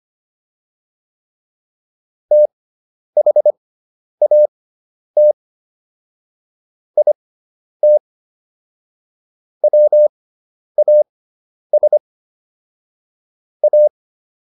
Morse code, heard as a keyed tone, about 25 wpm, spaced as for 10 wpm, 600 Hz. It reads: THAT IT WAS A